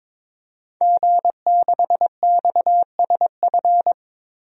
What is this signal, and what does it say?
Morse code, heard as a keyed tone, about 22 wpm, 700 Hz. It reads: G6XSF
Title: DXCC-100 Call Signs - 5 Characters - 22wpm